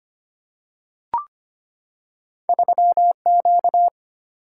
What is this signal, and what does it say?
Morse code, heard as a keyed tone, about 25 wpm, 700 Hz. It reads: 3Q
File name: Two Letter-Number Combinations - 25wpm